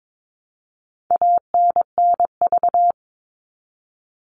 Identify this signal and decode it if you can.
Morse code, heard as a keyed tone, about 22 wpm, 700 Hz. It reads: ANNV
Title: All Cut Numbers - 4-Digit 22wpm